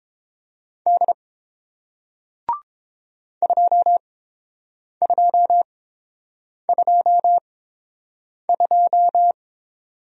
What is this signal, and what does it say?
Morse code, heard as a keyed tone, about 30 wpm, spaced as for 7 wpm, 700 Hz. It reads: D 2222